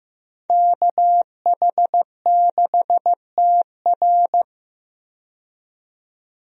Morse code, heard as a keyed tone, about 15 wpm, 700 Hz. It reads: KH6TR
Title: DXCC-100 Call Signs - 5 Characters - 15wpm